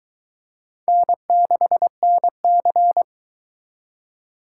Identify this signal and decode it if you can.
Morse code, heard as a keyed tone, about 22 wpm, 700 Hz. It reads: N6NC